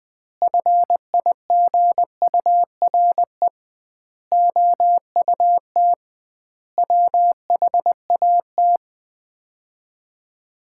Morse code, heard as a keyed tone, about 21 wpm, 700 Hz. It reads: FIGURE OUT WHAT
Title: 2x Word-Spacing - Sets of 5 Words - 20wpm